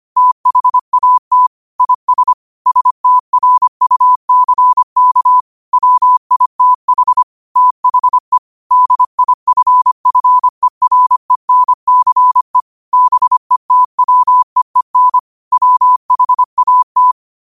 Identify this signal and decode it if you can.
Morse code, heard as a keyed tone, about 25 wpm, 1000 Hz. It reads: THAT IS STRUCK WITH THE DIFFERENCE BETWEEN WHAT
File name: qotdfiles / 25wpm